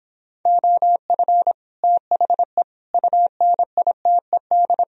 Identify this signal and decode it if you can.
Morse code, heard as a keyed tone, about 31 wpm, 700 Hz. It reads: OF THE UNITED